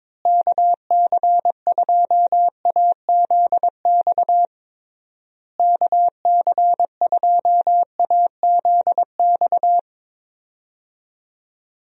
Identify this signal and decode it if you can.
Morse code, heard as a keyed tone, about 21 wpm, 700 Hz. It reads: KC2AZX KC2AZX
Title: USA Call Signs - 6 Characters - Repeated 3x - 22wpm